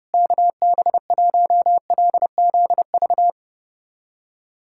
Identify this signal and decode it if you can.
Morse code, heard as a keyed tone, about 32 wpm, 700 Hz. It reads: KB1LZV
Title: USA Callsign DE Callsign - 6 Characters - 30wpm